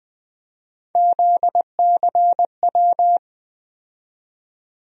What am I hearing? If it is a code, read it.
Morse code, heard as a keyed tone, about 20 wpm, 700 Hz. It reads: ZCW
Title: USA Call Sign Postfixes - 3 Letters 20wpm